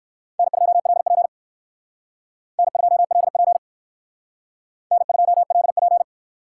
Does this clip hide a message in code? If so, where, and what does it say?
Yes: Morse code from 0.4 to 6.0 s, heard as a keyed tone, about 57 wpm, 700 Hz. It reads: N2LP N2LP N2LP